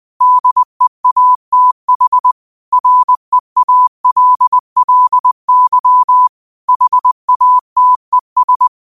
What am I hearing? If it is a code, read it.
Morse code, heard as a keyed tone, about 20 wpm, 1000 Hz. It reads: DEATH REALLY HATES